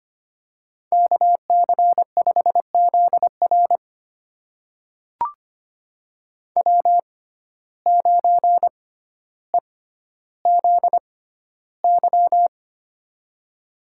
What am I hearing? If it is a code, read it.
Morse code, heard as a keyed tone, about 25 wpm, spaced as for 13 wpm, 700 Hz. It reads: KC5ZR W9EZY
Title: USA Call Signs - 5 Characters - 6x Character Spacing - 25wpm